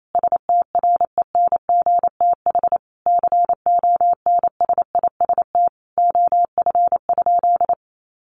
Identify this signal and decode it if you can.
Morse code, heard as a keyed tone, about 28 wpm, 700 Hz. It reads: STRENGTH CONSIST OF?
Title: Sentences from Top 1400 Words 28wpm